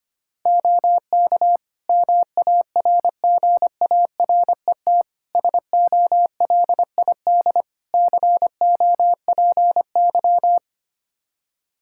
Morse code, heard as a keyed tone, about 25 wpm, 700 Hz. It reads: OK MARGARET SOLID COPY
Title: Practice QSO - Exchange 3 - More Info - 25wpm